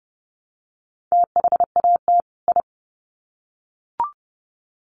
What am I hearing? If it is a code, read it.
Morse code, heard as a keyed tone, about 30 wpm, 700 Hz. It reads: THAT I
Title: Sets of 2 Words 30wpm